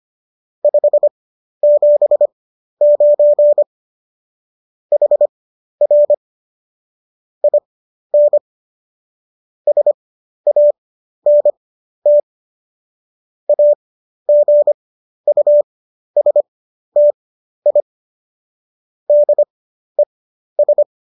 Morse code, heard as a keyed tone, about 25 wpm, spaced as for 12 wpm, 590 Hz. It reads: 579 HR IN SANT AGUSTI DES
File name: qso009 12